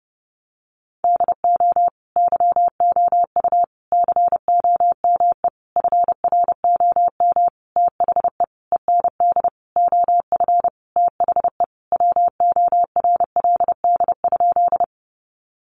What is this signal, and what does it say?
Morse code, heard as a keyed tone, about 28 wpm, 700 Hz. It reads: DO YOU COME FROM THE END OF THE WORLD?
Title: Sentences from Top 200 Words 30wpm